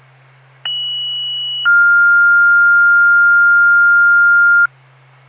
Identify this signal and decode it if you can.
Two-tone page, heard as a one-second tone, then a three-second tone, about 2950 then 1400 Hz. It is heard from 0.7 to 4.7 s.